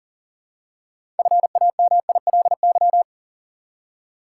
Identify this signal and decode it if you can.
Morse code, heard as a keyed tone, about 40 wpm, 700 Hz. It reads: FAMILY
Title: Six Letter Words - 40wpm